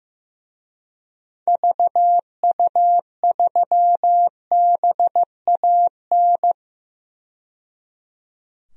Morse code, heard as a keyed tone, about 15 wpm, 700 Hz. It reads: VU3BAN
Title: DXCC-100 Call Signs - 6 Characters - 15wpm